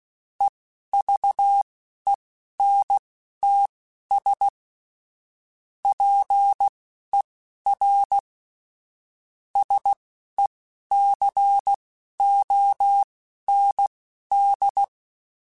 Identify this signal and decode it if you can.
Morse code, heard as a keyed tone, about 16 wpm, spaced as for 11 wpm, 800 Hz. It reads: EVENTS PER SECOND
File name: Technical Subjects / Technical Subjects-mp3s / XL @10